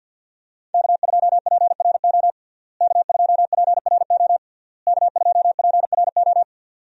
Morse code, heard as a keyed tone, about 52 wpm, 700 Hz. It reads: K2PRO K2PRO K2PRO
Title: USA Call Signs - 5 Characters - Repeated 3x - 50wpm